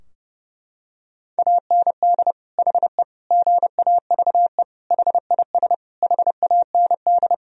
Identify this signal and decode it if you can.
Morse code, heard as a keyed tone, about 30 wpm, 700 Hz. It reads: AND HE GAVE HIS HAND
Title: Sentences from Top 1000 Words - Review Part 4 - 30wpm